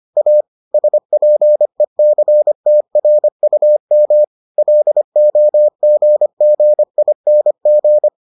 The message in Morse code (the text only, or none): A SPECTRUM LOGGING